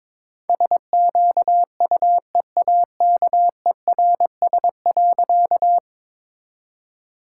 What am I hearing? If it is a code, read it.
Morse code, heard as a keyed tone, about 22 wpm, 700 Hz. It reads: SQUEAKERS.
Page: morse-code-ninja-practice-22wpm